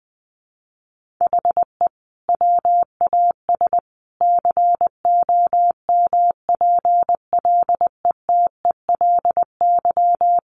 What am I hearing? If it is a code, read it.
Morse code, heard as a keyed tone, about 20 wpm, 700 Hz. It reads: HE WAS COMPLETELY
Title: Sentences from Top 1100 Words 20wpm